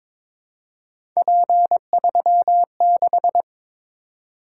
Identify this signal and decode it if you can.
Morse code, heard as a keyed tone, about 22 wpm, 700 Hz. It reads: P36
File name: practice / 22wpm / Three Letter-Number Combinations - 22wpm